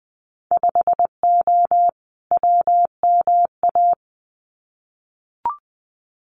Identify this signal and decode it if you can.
Morse code, heard as a keyed tone, about 20 wpm, 700 Hz. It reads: 5O WMA
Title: Winter Field Day Exchanges 20wpm